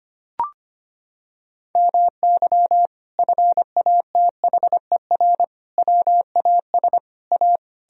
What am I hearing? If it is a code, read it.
Morse code, heard as a keyed tone, about 25 wpm, 700 Hz. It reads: MY FATHER WAS A
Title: Sentences from Top 1000 Words - Review Part 1 - 25wpm